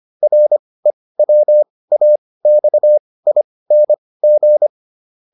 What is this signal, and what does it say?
Morse code, heard as a keyed tone, about 25 wpm, spaced as for 18 wpm, 600 Hz. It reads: REWAXING